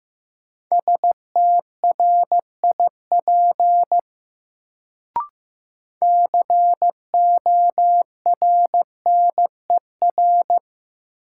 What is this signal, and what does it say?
Morse code, heard as a keyed tone, about 15 wpm, 700 Hz. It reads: STRIP CORNER
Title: Sentences from Top 1600 Words - New Word Drill - 15wpm